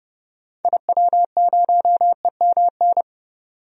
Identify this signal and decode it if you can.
Morse code, heard as a keyed tone, about 30 wpm, 700 Hz. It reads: IW0EMN
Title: DXCC-100 Call Signs - 6 Characters - 30wpm